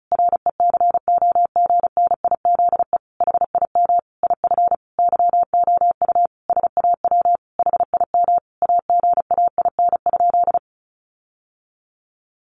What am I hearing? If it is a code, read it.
Morse code, heard as a keyed tone, about 34 wpm, 700 Hz. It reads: RECOGNIZE HIM IF YOU SAW HIM AGAIN?
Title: Sentences from Top 1400 Words 35wpm